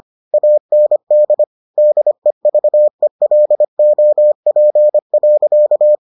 Morse code, heard as a keyed tone, about 25 wpm, 600 Hz. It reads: AND DEVELOP.